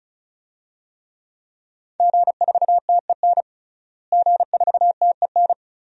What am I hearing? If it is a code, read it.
Morse code, heard as a keyed tone, about 36 wpm, 700 Hz. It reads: G4TEN G4TEN
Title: DXCC-100 Call Signs - 5 Characters - Repeated 2x - 35wpm